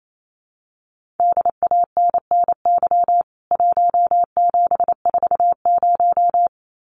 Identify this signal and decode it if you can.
Morse code, heard as a keyed tone, about 28 wpm, 700 Hz. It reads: DANNY 1740
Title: CWT Exchange 28wpm